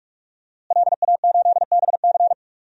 Morse code, heard as a keyed tone, about 46 wpm, 700 Hz. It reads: LA8BC
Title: DXCC-100 Call Signs - 5 Characters - Repeated 2x - 45wpm